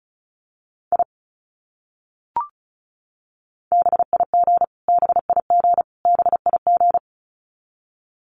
Morse code, heard as a keyed tone, about 34 wpm, 700 Hz. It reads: I BIG BIG BIG
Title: Top 300 Words Repeated 35wpm